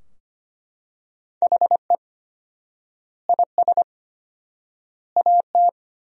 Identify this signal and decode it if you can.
Morse code, heard as a keyed tone, about 25 wpm, 700 Hz. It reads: HE IS AT
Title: 4x Word-Spacing - Sentences from Top 300 Words - 25wpm